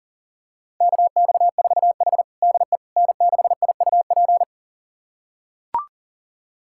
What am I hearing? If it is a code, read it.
Morse code, heard as a keyed tone, about 37 wpm, 700 Hz. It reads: KX4H DE N6IUP
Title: USA Callsign DE Callsign 40wpm